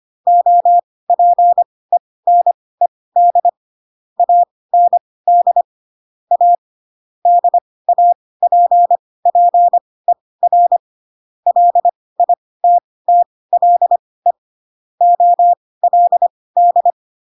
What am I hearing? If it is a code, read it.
Morse code, heard as a keyed tone, about 25 wpm, spaced as for 18 wpm, 700 Hz. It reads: OPENED AND A DAPPER LITTLE OLD
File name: Pinocchio / Pinocchio ch2 12 25 ik1hge